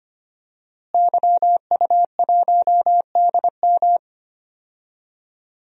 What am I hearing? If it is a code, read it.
Morse code, heard as a keyed tone, about 25 wpm, 700 Hz. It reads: YU1DM
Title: DXCC-100 Call Signs - 5 Characters - Repeated 3x - 25wpm